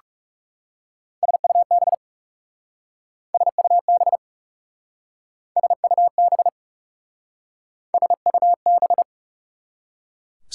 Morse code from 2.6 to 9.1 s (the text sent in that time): SUB SUB SUB